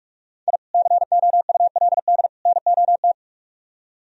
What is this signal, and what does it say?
Morse code, heard as a keyed tone, about 44 wpm, 700 Hz. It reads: I COULD NOT